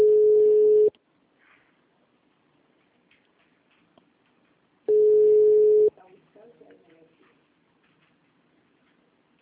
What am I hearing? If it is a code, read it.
Ringback tone (Europe, 425 Hz), with speech.